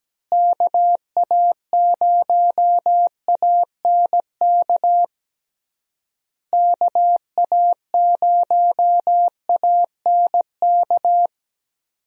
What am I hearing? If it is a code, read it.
Morse code, heard as a keyed tone, about 17 wpm, 700 Hz. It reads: KA0ANK KA0ANK
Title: USA Call Signs - 6 Characters - Repeated 3x - 17wpm